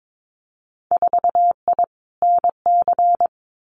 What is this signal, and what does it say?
Morse code, heard as a keyed tone, about 22 wpm, 700 Hz. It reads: 4I NC
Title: Winter Field Day Exchanges 22wpm